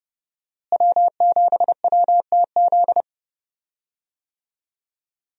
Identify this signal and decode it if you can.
Morse code, heard as a keyed tone, about 30 wpm, 700 Hz. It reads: W7WTZ
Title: USA Call Signs - 5 Characters - Repeated 3x - 30wpm